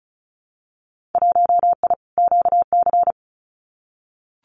Morse code, heard as a keyed tone, about 34 wpm, 700 Hz. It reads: 1I QC
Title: Winter Field Day Exchanges 35wpm